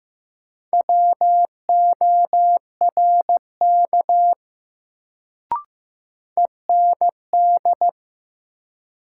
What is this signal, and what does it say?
Morse code, heard as a keyed tone, about 15 wpm, 700 Hz. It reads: WORK END
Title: Sentences from Top 200 Words - New Word Drill - 15wpm